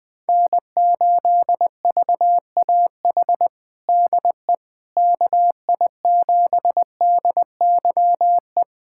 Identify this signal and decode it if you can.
Morse code, heard as a keyed tone, about 20 wpm, 700 Hz. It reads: N8VAH DE KI7DYE
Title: USA Callsign DE Callsign 20wpm